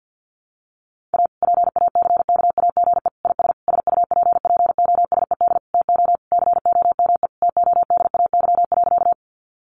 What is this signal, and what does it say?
Morse code, heard as a keyed tone, about 50 wpm, 700 Hz. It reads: A PACKAGE IS SUPPOSED TO COME TODAY.
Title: Easy Phrases 50wpm